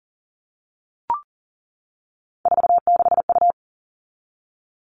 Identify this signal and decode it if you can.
Morse code, heard as a keyed tone, about 40 wpm, 700 Hz. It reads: 46U
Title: All Cut Numbers - 3-Digit 40wpm